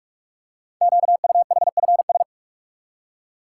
Morse code, heard as a keyed tone, about 44 wpm, 700 Hz. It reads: QUHFS
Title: Code Groups - 5 Letters - 45wpm